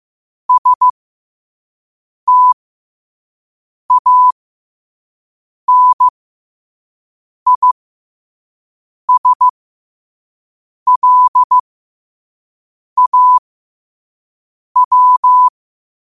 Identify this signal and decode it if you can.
Morse code, heard as a keyed tone, about 15 wpm, spaced as for 5 wpm, 1000 Hz. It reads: STANISLAW
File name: qotdfiles / 5wpm